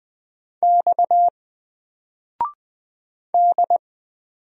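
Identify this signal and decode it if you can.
Morse code, heard as a keyed tone, about 20 wpm, 700 Hz. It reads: X D
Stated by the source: Single letter and number Morse drill at 20 wpm, from N4LCD.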